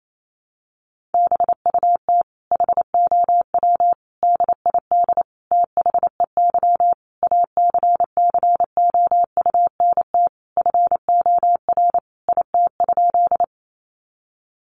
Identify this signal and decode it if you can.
Morse code, heard as a keyed tone, about 28 wpm, 700 Hz. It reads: BUT HOW DID THEY ACCOUNT FOR IT?
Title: Sentences from Top 900 Words 28wpm